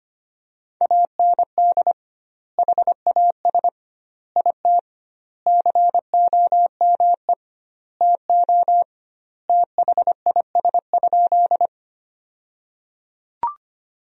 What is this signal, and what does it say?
Morse code, heard as a keyed tone, about 25 wpm, 700 Hz. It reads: AND HAS IT COME TO THIS?
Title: 2x Word-Spacing - Sentences from Top 100 Words - 25wpm